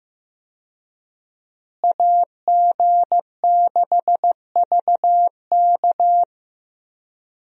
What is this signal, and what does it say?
Morse code, heard as a keyed tone, about 15 wpm, 700 Hz. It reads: AG6VK
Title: USA Call Signs - 5 Characters - Repeated 3x - 15wpm